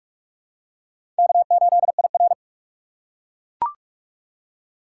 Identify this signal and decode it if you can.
Morse code, heard as a keyed tone, about 44 wpm, 700 Hz. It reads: K8IR
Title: USA Call Signs - 4 Characters - 3x Character Spacing - 45wpm